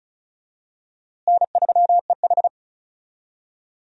Morse code, heard as a keyed tone, about 36 wpm, 700 Hz. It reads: N3EH
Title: USA Call Signs - 4 Characters - Repeated 3x - 35wpm